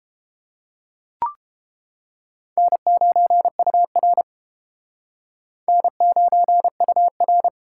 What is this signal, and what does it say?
Morse code, heard as a keyed tone, about 29 wpm, 700 Hz. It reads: N9UR N9UR